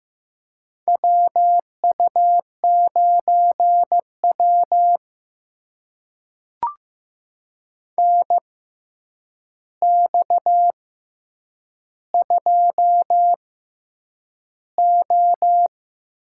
Morse code, heard as a keyed tone, about 15 wpm, spaced as for 8 wpm, 700 Hz. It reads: WU9W NX2O
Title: USA Call Signs - 4 Characters - 6x Character Spacing - 15wpm